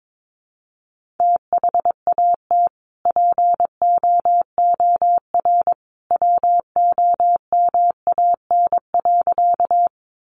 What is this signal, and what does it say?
Morse code, heard as a keyed tone, about 22 wpm, 700 Hz. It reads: THAT POOR WOMAN.
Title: Sentences from Top 1100 Words 22wpm